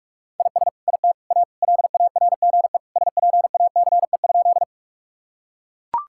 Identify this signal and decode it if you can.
Morse code, heard as a keyed tone, about 41 wpm, 700 Hz. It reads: IS IT A LARGE SPACE?
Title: Sentences from Top 1000 Words - Review Part 5 - 45wpm